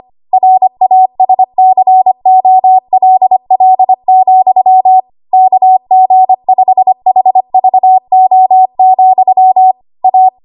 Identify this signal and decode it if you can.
Morse code, heard as a keyed tone, about 25 wpm, 750 Hz. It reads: RASCOLL, KG5HVO, A
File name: Archive / 25 WPM / 25WPM